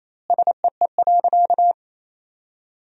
Morse code, heard as a keyed tone, about 28 wpm, 700 Hz. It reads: SEE.
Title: 2x Word-Spacing - Sentences from Top 500 Words - 28wpm